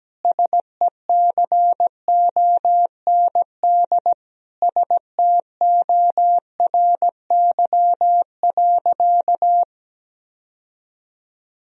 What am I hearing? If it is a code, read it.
Morse code, heard as a keyed tone, about 17 wpm, 700 Hz. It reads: SECOND STORY.
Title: Sentences from Top 1000 Words - Review Part 3 - 17wpm